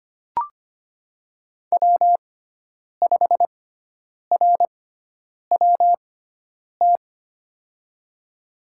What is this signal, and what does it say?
Morse code, heard as a keyed tone, about 25 wpm, spaced as for 9 wpm, 700 Hz. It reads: W5RWT
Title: USA Call Signs - 5 Characters - 6x Character Spacing - 25wpm